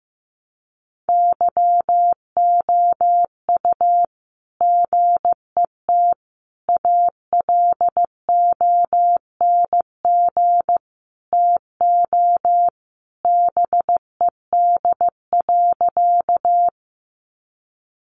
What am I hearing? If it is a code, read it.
Morse code, heard as a keyed tone, about 15 wpm, 700 Hz. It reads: YOU GET ALONG TO BED.
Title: Sentences from Top 700 Words 15wpm